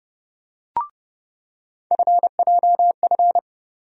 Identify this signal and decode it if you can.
Morse code, heard as a keyed tone, about 30 wpm, 700 Hz. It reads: FJF
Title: USA Call Sign Postfixes - 3 Letters 30wpm